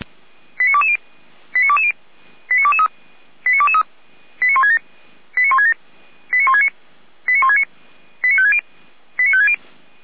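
Five-tone selective call (ZVEI), heard as quick runs of five tones, tones of about 70 ms. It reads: 89220 89220 89223 89223 89167 89167 89169 89169 89560 89560
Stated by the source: ZVEI five-tone selective call received on an SDR.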